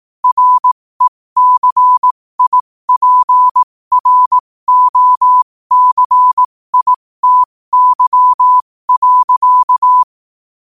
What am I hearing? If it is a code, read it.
Morse code, heard as a keyed tone, about 18 wpm, spaced as for 15 wpm, 1000 Hz. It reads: RECIPROCITY.
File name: qotdfiles / 15wpm